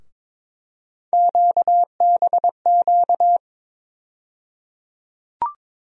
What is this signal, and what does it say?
Morse code, heard as a keyed tone, about 23 wpm, 700 Hz. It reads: QBQ
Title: USA Call Sign Postfixes - 3 Letters 22wpm